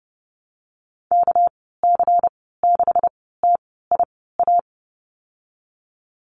Morse code, heard as a keyed tone, about 30 wpm, spaced as for 17 wpm, 700 Hz. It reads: KC6TIA
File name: Lesson 37g - Letter x - Call Signs 30wpm